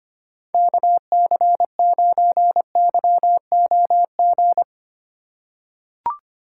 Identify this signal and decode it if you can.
Morse code, heard as a keyed tone, about 25 wpm, 700 Hz. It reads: KC9YOG
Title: USA Call Signs - 6 Characters - 3x Character Spacing - 25wpm